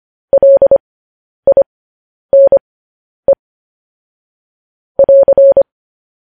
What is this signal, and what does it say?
Morse code, heard as a keyed tone, about 25 wpm, spaced as for 10 wpm, 550 Hz. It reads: LINE <AR>
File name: podcast / News headlines-1x-25x10